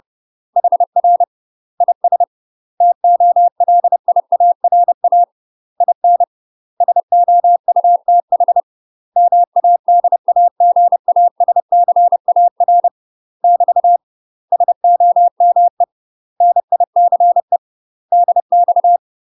Morse code, heard as a keyed tone, about 30 wpm, 700 Hz. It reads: HR IS TOLIARA IN SOUTH MADAGASCAR <BT> SOME NICE DX
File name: UK MASTER QSOs TRAINING / 30WPM / UK MASTER QSOs 0008